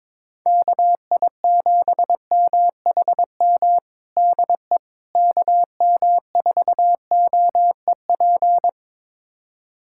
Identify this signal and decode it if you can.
Morse code, heard as a keyed tone, about 22 wpm, 700 Hz. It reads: KI7MHM DE KM4OEP
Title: USA Callsign DE Callsign - 6 Characters - 22wpm